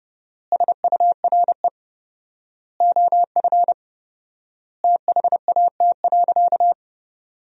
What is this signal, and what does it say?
Morse code, heard as a keyed tone, about 30 wpm, 700 Hz. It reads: SURE OF THAT.
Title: 4x Word-Spacing - Sentences from Top 300 Words - 30wpm